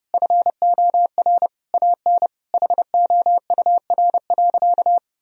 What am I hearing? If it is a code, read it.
Morse code, heard as a keyed tone, about 30 wpm, 700 Hz. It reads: FOR AN HOUR.
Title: Sentences from Top 1000 Words - Review Part 4 - 30wpm